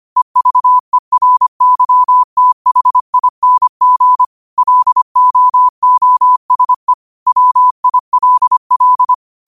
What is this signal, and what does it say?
Morse code, heard as a keyed tone, about 25 wpm, 1000 Hz. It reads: EVERYTHING LOOSE WILL